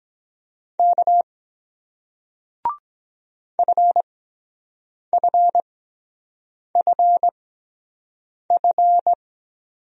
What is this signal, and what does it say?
Morse code, heard as a keyed tone, about 23 wpm, 700 Hz. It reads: K F F F F